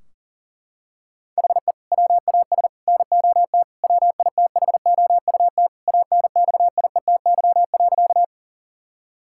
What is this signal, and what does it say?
Morse code, heard as a keyed tone, about 40 wpm, 700 Hz. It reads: HE WAS NOT WITHOUT ANXIETY.
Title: Sentences from Top 1600 Words 40wpm